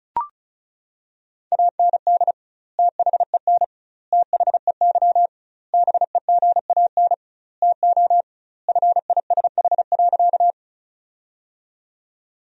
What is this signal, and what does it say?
Morse code, heard as a keyed tone, about 36 wpm, 700 Hz. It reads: AND THEN THEY BEGAN TO FISH.
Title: 2x Word-Spacing - Sentences from Top 300 Words - 35wpm